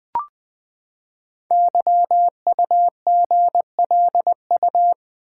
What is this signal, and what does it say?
Morse code, heard as a keyed tone, about 20 wpm, 700 Hz. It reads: YUGLU
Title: Code Groups - 5 Letter:Numbers - 20wpm